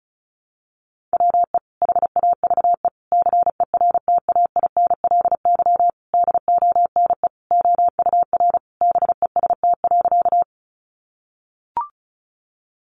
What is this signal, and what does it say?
Morse code, heard as a keyed tone, about 34 wpm, 700 Hz. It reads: WE HAVE CERTAINLY DONE OUR BEST.
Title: Sentences from Top 900 Words 35wpm